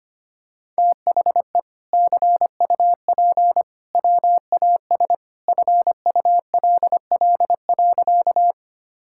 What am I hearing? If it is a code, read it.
Morse code, heard as a keyed tone, about 25 wpm, 700 Hz. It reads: THE CUP WAS FULL.
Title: Sentences from Top 1600 Words 25wpm